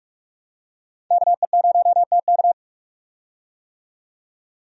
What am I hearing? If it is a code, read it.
Morse code, heard as a keyed tone, about 47 wpm, 700 Hz. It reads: KE0TK
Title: USA Call Signs - 5 Characters - 45wpm